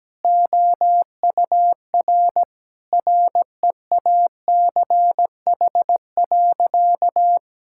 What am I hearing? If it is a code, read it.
Morse code, heard as a keyed tone, about 17 wpm, 700 Hz. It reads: OUR REACH.
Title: Sentences from Top 1000 Words - Review Part 6 - 17wpm